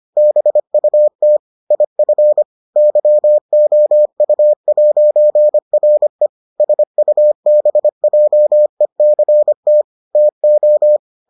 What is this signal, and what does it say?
Morse code, heard as a keyed tone, about 25 wpm, 600 Hz. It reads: BUT IF YOU'RE SUBJECT TO